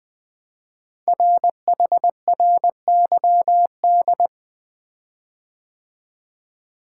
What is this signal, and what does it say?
Morse code, heard as a keyed tone, about 20 wpm, 700 Hz. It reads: RHRYD
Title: Code Groups - 5 Letters - 20wpm